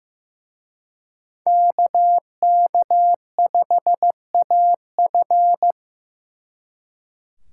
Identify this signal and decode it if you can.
Morse code, heard as a keyed tone, about 18 wpm, 700 Hz. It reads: KK5AF
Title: Speed Racing - USA 2x2 Call Signs 15wpm